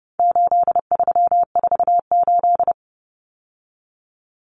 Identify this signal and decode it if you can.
Morse code, heard as a keyed tone, about 30 wpm, 700 Hz. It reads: 8348
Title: Common Cut Numbers - 4-Digit 30wpm